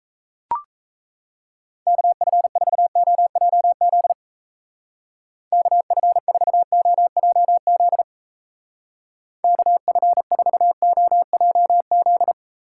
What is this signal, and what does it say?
Morse code, heard as a keyed tone, about 35 wpm, 700 Hz. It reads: KF4OJZ KF4OJZ KF4OJZ
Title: Speed Racing - USA Call Signs 28wpm